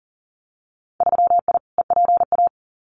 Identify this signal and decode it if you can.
Morse code, heard as a keyed tone, about 40 wpm, 700 Hz. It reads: 3I EPA
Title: Winter Field Day Exchanges 40wpm